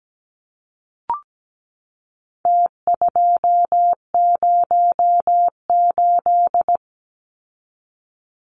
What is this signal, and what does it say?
Morse code, heard as a keyed tone, about 17 wpm, 700 Hz. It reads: T208
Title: Numbers - Contest Style - 4-Digit 17wpm